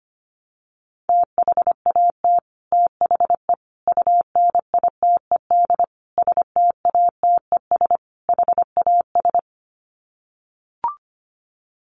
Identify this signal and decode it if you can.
Morse code, heard as a keyed tone, about 25 wpm, 700 Hz. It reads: THAT THE UNITED STATES HAS
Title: Sets of 5 Words Encore 25wpm